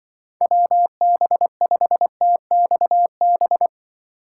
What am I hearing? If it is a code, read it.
Morse code, heard as a keyed tone, about 24 wpm, 700 Hz. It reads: WB5TXB
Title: Speed Racing - USA 2x3 Call Signs 20wpm